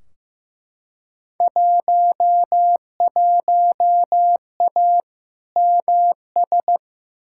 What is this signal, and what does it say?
Morse code, heard as a keyed tone, about 15 wpm, 700 Hz. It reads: 11A MS
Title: Field Day Exchanges Encore - 15wpm